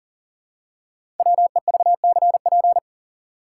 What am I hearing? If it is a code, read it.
Morse code, heard as a keyed tone, about 40 wpm, 700 Hz. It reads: WEVCP
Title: Code Groups - 5 Letter:Numbers - 40wpm